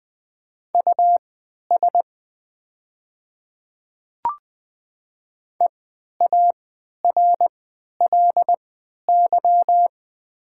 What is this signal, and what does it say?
Morse code, heard as a keyed tone, about 20 wpm, spaced as for 11 wpm, 690 Hz. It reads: US EARLY
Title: 3x Character Spacing - Top 500 Words - 20wpm